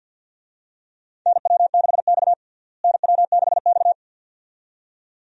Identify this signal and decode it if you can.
Morse code, heard as a keyed tone, about 53 wpm, 700 Hz. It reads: NW6X NW6X
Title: USA Call Signs - 4 Characters - Repeated 2x - 50wpm